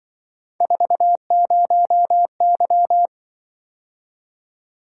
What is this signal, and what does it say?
Morse code, heard as a keyed tone, about 24 wpm, 700 Hz. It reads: 40Y
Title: Speed-Racing - Three Letter-Number Combinations - 20wpm